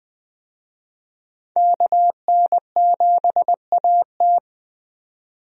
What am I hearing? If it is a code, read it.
Morse code, heard as a keyed tone, about 20 wpm, 700 Hz. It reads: KN7AT
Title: USA Call Signs - 5 Characters - 20wpm